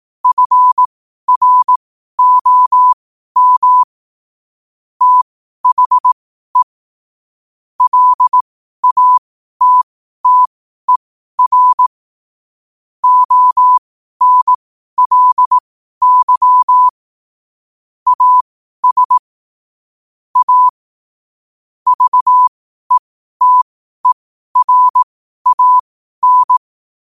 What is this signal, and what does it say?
Morse code, heard as a keyed tone, about 18 wpm, spaced as for 12 wpm, 1000 Hz. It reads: FROM THE LATTER ONLY AS A VETERAN
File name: qotdfiles / 12wpm